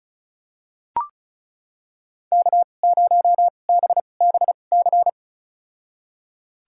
Morse code, heard as a keyed tone, about 36 wpm, spaced as for 25 wpm, 700 Hz. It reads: K0BBC
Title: USA Call Signs - 5 Characters - 2x Character Spacing - 35wpm